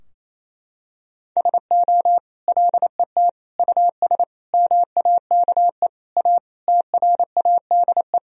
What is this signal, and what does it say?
Morse code, heard as a keyed tone, about 28 wpm, 700 Hz. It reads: SO LET US MAKE A TRADE
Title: Sentences from Top 1000 Words - Review Part 4 - 28wpm